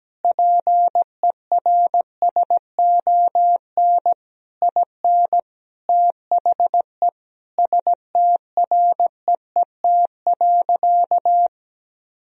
Morse code, heard as a keyed tone, about 17 wpm, 700 Hz. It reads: PERSON IN THE STREET.
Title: Sentences from Top 1000 Words - Review Part 3 - 17wpm